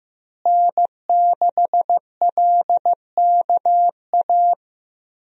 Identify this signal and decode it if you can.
Morse code, heard as a keyed tone, about 15 wpm, 700 Hz. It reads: N6LKA
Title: USA Call Signs - 5 Characters - Repeated 3x - 15wpm